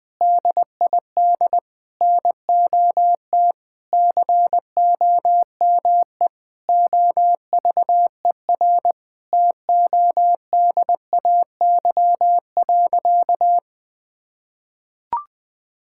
Morse code, heard as a keyed tone, about 20 wpm, 700 Hz. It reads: DID NOT COME OVER TODAY.
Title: Sentences with 4-Word Phrases - Spoken Beginning - 20wpm